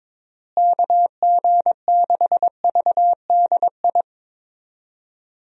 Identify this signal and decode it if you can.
Morse code, heard as a keyed tone, about 22 wpm, 700 Hz. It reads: KG6VDI
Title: USA Call Signs - 6 Characters - Repeated 3x - 22wpm